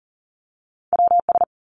Morse code, heard as a keyed tone, about 42 wpm, 700 Hz. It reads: WS
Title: Two Challenging Letter-Number Combinations 40wpm